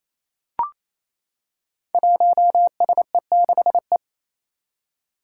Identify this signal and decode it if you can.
Morse code, heard as a keyed tone, about 28 wpm, 700 Hz. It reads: 1SE6E
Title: Five Challenging Letter-Number Combinations 28wpm